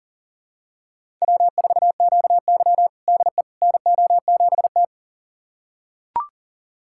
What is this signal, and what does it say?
Morse code, heard as a keyed tone, about 40 wpm, 700 Hz. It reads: W4QY DE NO7T
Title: USA Callsign DE Callsign - 4 Characters - 40wpm